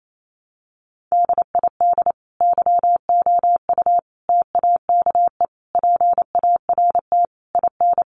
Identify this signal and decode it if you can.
Morse code, heard as a keyed tone, about 28 wpm, 700 Hz. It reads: DID YOU TAKE PART IN
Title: Sentences from Top 1200 Words 28wpm